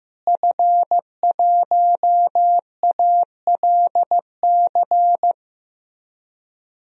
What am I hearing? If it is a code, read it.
Morse code, heard as a keyed tone, about 15 wpm, 700 Hz. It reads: F1ALC
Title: DXCC-100 Call Signs - 5 Characters - Repeated 3x - 15wpm